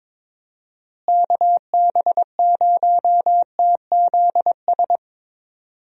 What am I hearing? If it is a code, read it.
Morse code, heard as a keyed tone, about 22 wpm, 700 Hz. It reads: KB0TZS
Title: USA Call Signs - 6 Characters - Repeated 3x - 22wpm